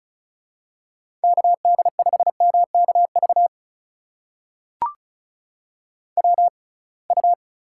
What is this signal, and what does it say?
Morse code, heard as a keyed tone, about 36 wpm, spaced as for 27 wpm, 700 Hz. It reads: KD5MKV WU
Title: USA Call Signs - 6x Character Spacing - 35wpm